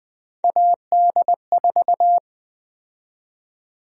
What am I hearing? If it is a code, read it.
Morse code, heard as a keyed tone, about 20 wpm, 700 Hz. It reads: AD4
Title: USA Call Sign Prefixes - 20wpm